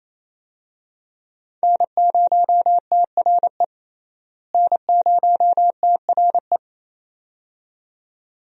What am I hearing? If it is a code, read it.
Morse code, heard as a keyed tone, about 28 wpm, 700 Hz. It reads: N0TRE N0TRE
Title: USA Call Signs - 5 Characters - Repeated 2x - 28wpm